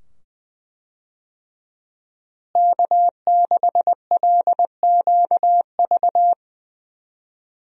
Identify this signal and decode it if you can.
Morse code, heard as a keyed tone, about 20 wpm, 700 Hz. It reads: K6LQV
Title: USA Call Signs - 5 Characters - Repeated 2x - 20wpm